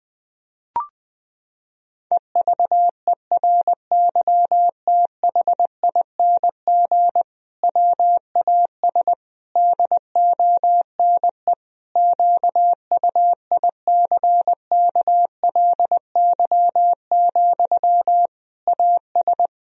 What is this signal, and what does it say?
Morse code, heard as a keyed tone, about 20 wpm, 700 Hz. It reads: EVERYTHING WAS DONE QUICKLY, AS
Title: Sentences from Top 1000 Words - Review Part 2 - 20wpm